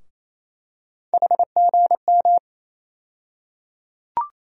Morse code, heard as a keyed tone, about 28 wpm, 700 Hz. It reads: HGM
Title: USA Call Sign Postfixes - 3 Letters 28wpm